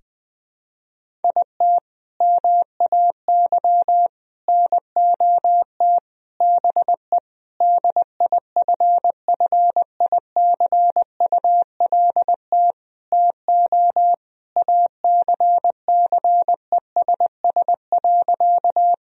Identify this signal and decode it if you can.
Morse code, heard as a keyed tone, about 20 wpm, 700 Hz. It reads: IT MAY NOT BE DIFFICULT TO ACCESS.
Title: Sentences from Top 1000 Words - Review Part 4 - 20wpm